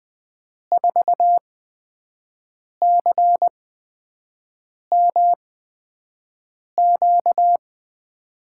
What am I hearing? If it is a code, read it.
Morse code, heard as a keyed tone, about 20 wpm, spaced as for 5 wpm, 700 Hz. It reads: 4CMQ
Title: USA Call Signs - 8x Character Spacing - 20wpm